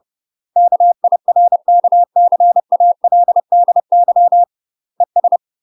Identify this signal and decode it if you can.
Morse code, heard as a keyed tone, about 30 wpm, 700 Hz. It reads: KIRKCALDY ES
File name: UK MASTER QSOs TRAINING / 30WPM / UK MASTER QSOs 0008